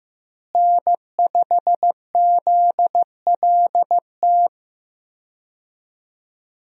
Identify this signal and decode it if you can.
Morse code, heard as a keyed tone, about 15 wpm, 700 Hz. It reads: N5ZLT
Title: USA Call Signs - 5 Characters - Repeated 2x - 15wpm